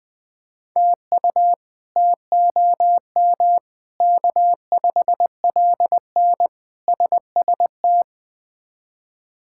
Morse code, heard as a keyed tone, about 21 wpm, 700 Hz. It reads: TU TOM K5LN SST
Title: K1USN SST QSO Practice - 20wpm